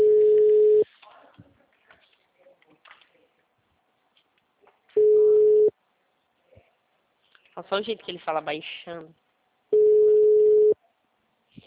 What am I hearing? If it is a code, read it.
Ringback tone (Europe, 425 Hz), with speech.
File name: out- -2014- - - .9815